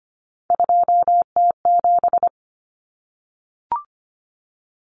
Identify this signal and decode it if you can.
Morse code, heard as a keyed tone, about 25 wpm, 700 Hz. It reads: 2T7
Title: Common Cut Numbers - 3-Digit 25wpm